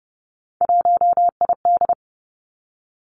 Morse code, heard as a keyed tone, about 30 wpm, 700 Hz. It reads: 1ID
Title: Three Challenging Letter-Number Combinations 30wpm